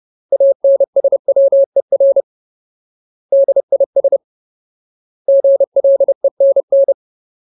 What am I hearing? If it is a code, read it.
Morse code, heard as a keyed tone, about 30 wpm, 550 Hz. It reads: ANSWER DIS GLENN
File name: Ultimate Word List / 30WPM Ultimate